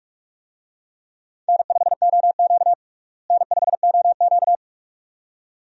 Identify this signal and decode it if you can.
Morse code, heard as a keyed tone, about 46 wpm, 700 Hz. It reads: N5OQ N5OQ
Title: USA Call Signs - 4 Characters - Repeated 2x - 45wpm